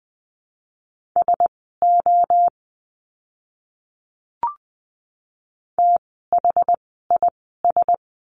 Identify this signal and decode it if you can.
Morse code, heard as a keyed tone, about 20 wpm, spaced as for 14 wpm, 700 Hz. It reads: SO THIS
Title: 230 Lesson 37f - Letter x - Words 20wpm